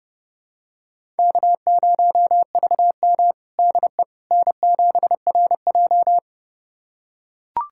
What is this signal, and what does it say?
Morse code, heard as a keyed tone, about 30 wpm, 700 Hz. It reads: K0VM DE N7RJ
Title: USA Callsign DE Callsign - 4 Characters - 30wpm